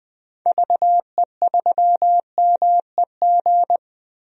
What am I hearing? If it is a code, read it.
Morse code, heard as a keyed tone, about 20 wpm, 700 Hz. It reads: VE3MEG
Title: DXCC-100 Call Signs - 6 Characters - 20wpm